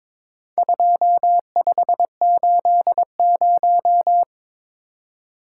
Morse code, heard as a keyed tone, about 22 wpm, 700 Hz. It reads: 2580
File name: practice / 22wpm / Four Letter-Number Combinations - 22wpm